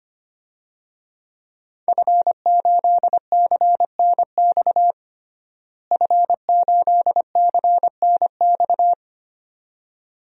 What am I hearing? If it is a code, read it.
Morse code, heard as a keyed tone, about 25 wpm, 700 Hz. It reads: F8CNX F8CNX
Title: DXCC-100 Call Signs - 5 Characters - Repeated 2x - 25wpm